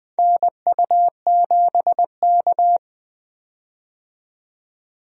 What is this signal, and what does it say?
Morse code, heard as a keyed tone, about 20 wpm, 700 Hz. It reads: NU7K
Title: USA Call Signs - 4 Characters - 20wpm